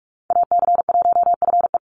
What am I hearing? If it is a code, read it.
Morse code, heard as a keyed tone, about 44 wpm, 700 Hz. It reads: AC1FE
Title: Call Signs Repeated 3x - 45wpm